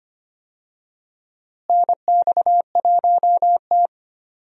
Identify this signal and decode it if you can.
Morse code, heard as a keyed tone, about 25 wpm, 700 Hz. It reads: NX1T
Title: USA Call Signs - 4 Characters - Repeated 3x - 25wpm